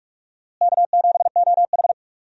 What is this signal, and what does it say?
Morse code, heard as a keyed tone, about 46 wpm, 700 Hz. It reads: K7OH
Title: USA Call Signs - 4 Characters - Repeated 3x - 45wpm